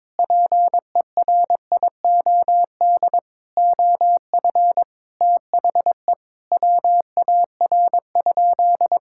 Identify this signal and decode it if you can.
Morse code, heard as a keyed tone, about 22 wpm, 700 Hz. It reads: PERIOD OF THE WAR?
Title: Sentences from Top 1000 Words - Review Part 6 - 22wpm